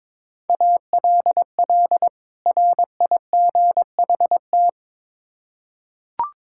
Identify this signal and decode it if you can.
Morse code, heard as a keyed tone, about 22 wpm, 700 Hz. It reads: ALL RIGHT
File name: Build A Phrase - 2-Word Phrases - 22wpm